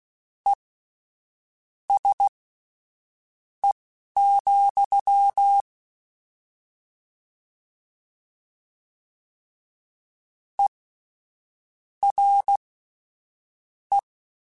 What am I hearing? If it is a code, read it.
Morse code, heard as a keyed tone, about 16 wpm, spaced as for 6 wpm, 800 Hz. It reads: ESE, ERE
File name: General / General-mp3s / Suffixes sp @10